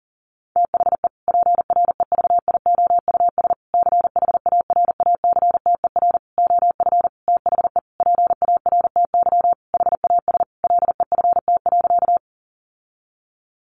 Morse code, heard as a keyed tone, about 40 wpm, 700 Hz. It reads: THE PREVIOUS CHARACTER OF THE PARTY HAS LEFT.
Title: Sentences from Top 1500 Words 40wpm